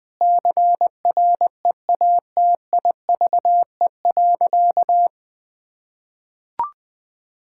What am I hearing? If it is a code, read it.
Morse code, heard as a keyed tone, about 20 wpm, 700 Hz. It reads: CREATIVE.
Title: Sentences from Top 1000 Words - Review Part 5 - 20wpm